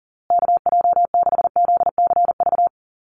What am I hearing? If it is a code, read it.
Morse code, heard as a keyed tone, about 40 wpm, 700 Hz. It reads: KJ6ZCV
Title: USA Call Signs Repeated 3x 40wpm